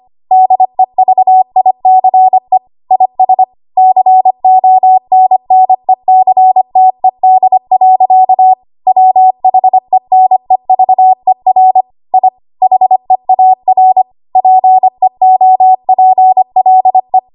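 Morse code, heard as a keyed tone, about 25 wpm, 750 Hz. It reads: DEVICE IS CONNECTED. WHENEVER I HEAR PEOPLE
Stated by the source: ARRL 25 WPM code practice archive.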